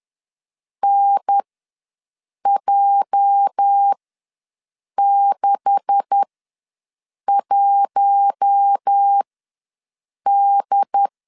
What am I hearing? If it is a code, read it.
Morse code, heard as a keyed tone, about 10 wpm, spaced as for 6 wpm, 800 Hz. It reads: NJ61D
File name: LEZIONI / 10 SIGNS / 10C DOT ws 10wpm 4213